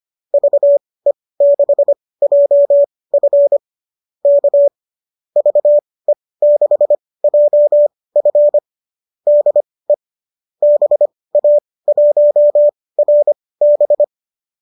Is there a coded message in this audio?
Yes: Morse code from 0.3 to 14.0 s, heard as a keyed tone, about 25 wpm, spaced as for 18 wpm, 570 Hz. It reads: VE6JF K VE6JF DE BA1RB